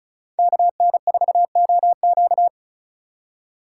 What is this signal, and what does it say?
Morse code, heard as a keyed tone, about 36 wpm, 700 Hz. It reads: KN4OQ
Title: USA Call Signs - 4x Character Spacing - 35wpm